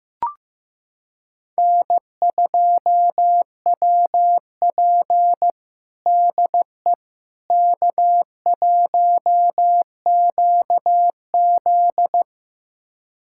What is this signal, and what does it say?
Morse code, heard as a keyed tone, about 16 wpm, 700 Hz. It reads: N2WP DE K1QZ
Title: USA Callsign DE Callsign - 4 Characters - 15wpm